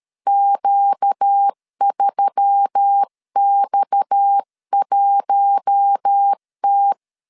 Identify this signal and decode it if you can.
Morse code, heard as a keyed tone, about 13 wpm, 800 Hz. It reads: Q3X1T
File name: CW / DET / 1 6 1500C S C 12wpm 3533